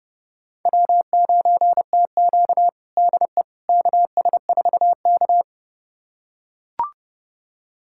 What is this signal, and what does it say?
Morse code, heard as a keyed tone, about 29 wpm, 700 Hz. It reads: W9TQ DE KS4K
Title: USA Callsign DE Callsign - 4 Characters - 30wpm